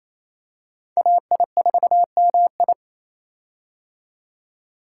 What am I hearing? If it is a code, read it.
Morse code, heard as a keyed tone, about 27 wpm, 700 Hz. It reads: AI4MI
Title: K1USN SST QSO Practice - 28wpm